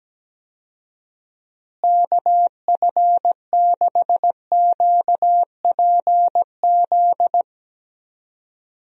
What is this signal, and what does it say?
Morse code, heard as a keyed tone, about 17 wpm, 700 Hz. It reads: KF6QPZ
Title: USA Call Signs - 6 Characters - Repeated 3x - 17wpm